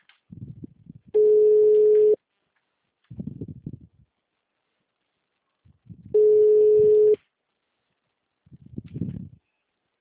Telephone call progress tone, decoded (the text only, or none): ringback tone